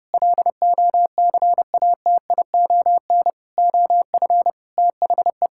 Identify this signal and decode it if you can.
Morse code, heard as a keyed tone, about 30 wpm, 700 Hz. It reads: LOCATION OF THE
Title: Sentences from Top 1000 Words - Review Part 6 - 30wpm